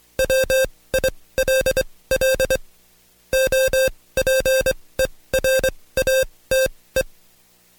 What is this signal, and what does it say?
Morse code, heard as a keyed tone, about 24 wpm, spaced as for 18 wpm, 530 Hz. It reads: WILL OPERATE
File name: Code Quick / AudioDownld2 / Seg135-Stepper8 17WPM